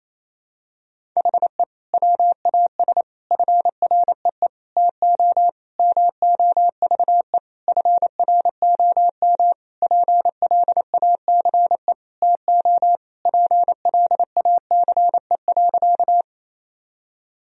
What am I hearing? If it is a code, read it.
Morse code, heard as a keyed tone, about 28 wpm, 700 Hz. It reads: HE WAS FREE TO MOVE FROM PLACE TO PLACE.
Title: Sentences from Top 1000 Words - Review Part 1 - 28wpm